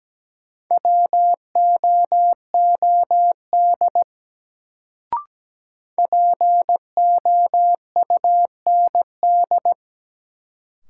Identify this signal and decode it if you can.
Morse code, heard as a keyed tone, about 17 wpm, 700 Hz. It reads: WOOD POUND
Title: Sentences from Top 1000 Words - Review Part 2 - New Word Drill - 17wpm